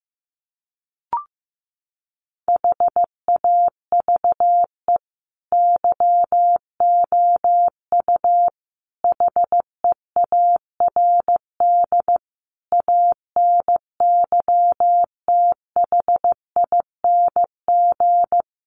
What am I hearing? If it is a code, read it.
Morse code, heard as a keyed tone, about 15 wpm, 700 Hz. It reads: HAVE YOU HEARD ANYTHING
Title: Sentences from Top 1000 Words 15wpm